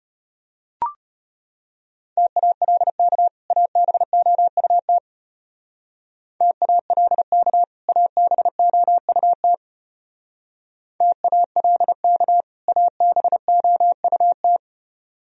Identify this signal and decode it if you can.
Morse code, heard as a keyed tone, about 33 wpm, 700 Hz. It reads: TALK ABOUT TALK ABOUT TALK ABOUT